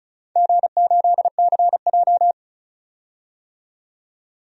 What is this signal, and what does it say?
Morse code, heard as a keyed tone, about 36 wpm, 700 Hz. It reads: G8CJ
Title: DXCC-100 Call Signs - 4 Characters - 35wpm